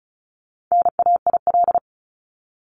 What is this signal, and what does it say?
Morse code, heard as a keyed tone, about 34 wpm, 700 Hz. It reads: NAIL
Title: Top 1501-1750 Words 35wpm